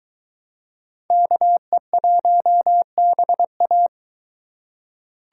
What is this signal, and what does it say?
Morse code, heard as a keyed tone, about 22 wpm, 700 Hz. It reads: KE1BA